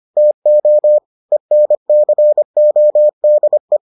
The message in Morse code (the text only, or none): TO ENCODE